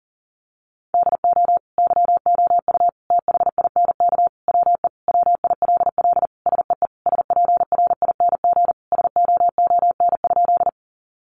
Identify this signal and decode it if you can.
Morse code, heard as a keyed tone, about 40 wpm, 700 Hz. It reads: DO YOU THINK WE WILL SEE SPRING SOON?
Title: Sentences from Top 1200 Words 40wpm